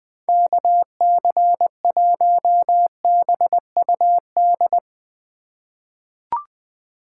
Morse code, heard as a keyed tone, about 21 wpm, 700 Hz. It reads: KC1BUD
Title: USA Call Signs - 6 Characters - 3x Character Spacing - 20wpm